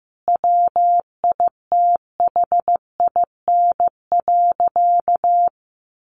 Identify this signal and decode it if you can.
Morse code, heard as a keyed tone, about 15 wpm, 700 Hz. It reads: WITHIN.
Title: Sentences from Top 700 Words 15wpm